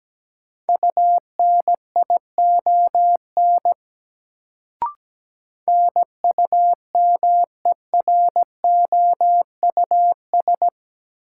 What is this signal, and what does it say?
Morse code, heard as a keyed tone, about 17 wpm, 700 Hz. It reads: UNION NUMEROUS
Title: Sentences from Top 1400 Words - New Word Drill Alt - 17wpm